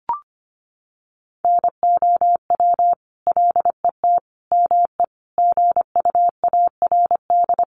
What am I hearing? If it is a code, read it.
Morse code, heard as a keyed tone, about 25 wpm, 700 Hz. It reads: NOW LET ME GUARD
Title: Sentences from Top 700 Words 25wpm